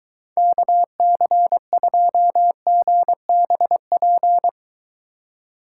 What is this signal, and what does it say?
Morse code, heard as a keyed tone, about 23 wpm, 700 Hz. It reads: KC2GBP